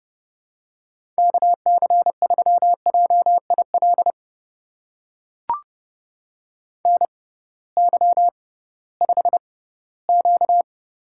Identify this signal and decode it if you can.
Morse code, heard as a keyed tone, about 30 wpm, spaced as for 17 wpm, 700 Hz. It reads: KC3JIL NY5Q
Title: USA Call Signs - 6x Character Spacing - 30wpm